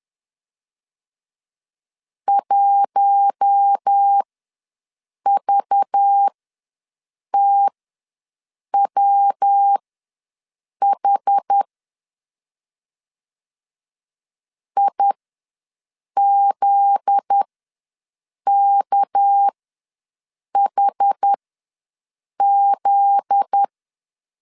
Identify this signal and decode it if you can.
Morse code, heard as a keyed tone, about 10 wpm, spaced as for 6 wpm, 800 Hz. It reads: 1VTWH IZKHZ
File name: LEZIONI / 7 GF9Z / 7 GF9Z ws 10wpm 3919